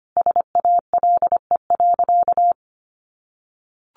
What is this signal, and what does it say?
Morse code, heard as a keyed tone, about 25 wpm, 700 Hz. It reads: SALE.
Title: Sentences from Top 1400 Words 25wpm